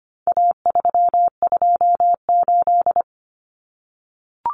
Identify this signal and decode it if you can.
Morse code, heard as a keyed tone, about 25 wpm, 700 Hz. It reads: A328
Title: Common Cut Numbers - 4-Digit 25wpm